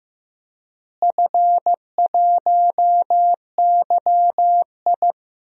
Morse code, heard as a keyed tone, about 16 wpm, 700 Hz. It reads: F1YI
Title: DXCC-100 Call Signs - 4 Characters - Repeated 3x - 15wpm